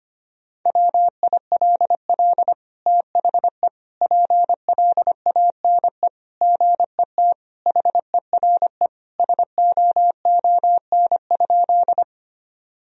Morse code, heard as a keyed tone, about 25 wpm, 700 Hz. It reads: WILL THE PLANE GET HERE SOON?
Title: Sentences from Top 1000 Words - Review Part 3 - 25wpm